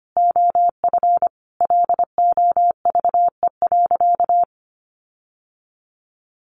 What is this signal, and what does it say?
Morse code, heard as a keyed tone, about 25 wpm, 700 Hz. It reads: OF LOVE.
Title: Sentences from Top 1300 Words 25wpm